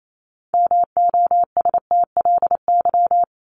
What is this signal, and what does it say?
Morse code, heard as a keyed tone, about 28 wpm, 700 Hz. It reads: MOSTLY
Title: Top 1001-1250 Words 28wpm